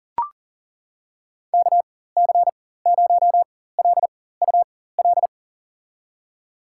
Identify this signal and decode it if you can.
Morse code, heard as a keyed tone, about 38 wpm, spaced as for 18 wpm, 700 Hz. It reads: KC0LUL
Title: USA Call Signs - 6 Characters - 4x Character Spacing - 40wpm